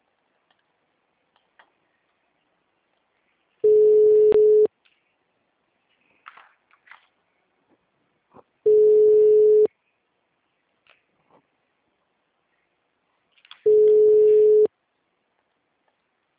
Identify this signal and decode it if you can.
Ringback tone (Europe, 425 Hz).